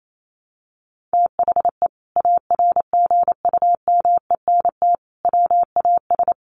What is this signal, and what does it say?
Morse code, heard as a keyed tone, about 28 wpm, 700 Hz. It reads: THE ARGUMENT WAS
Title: Sentences from Top 1400 Words 28wpm